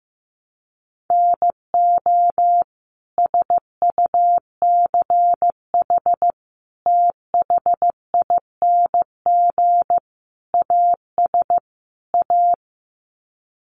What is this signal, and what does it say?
Morse code, heard as a keyed tone, about 15 wpm, 700 Hz. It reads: NO SUCH THING AS A
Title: Sets of 5 Words Encore 15wpm